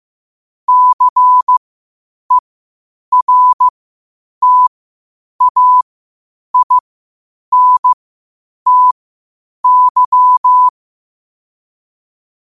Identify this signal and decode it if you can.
Morse code, heard as a keyed tone, about 15 wpm, spaced as for 8 wpm, 1000 Hz. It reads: CERTAINTY
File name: qotdfiles / 8wpm